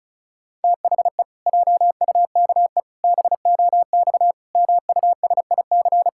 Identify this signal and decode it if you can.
Morse code, heard as a keyed tone, about 36 wpm, 700 Hz. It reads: THE JUKE BOX MUSIC
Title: Pangram Challenge v2 - 35wpm